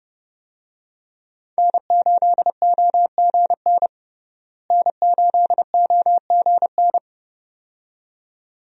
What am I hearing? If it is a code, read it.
Morse code, heard as a keyed tone, about 31 wpm, 700 Hz. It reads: N8OGN N8OGN
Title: USA Call Signs - 5 Characters - Repeated 2x - 30wpm